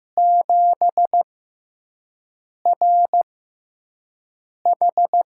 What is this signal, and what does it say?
Morse code, heard as a keyed tone, about 15 wpm, spaced as for 5 wpm, 700 Hz. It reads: 7RH
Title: USA Call Signs - 6x Character Spacing - 15wpm